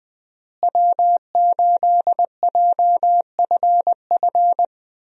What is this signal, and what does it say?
Morse code, heard as a keyed tone, about 21 wpm, 700 Hz. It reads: W8JFF
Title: USA Call Signs - 5 Characters - Repeated 3x - 20wpm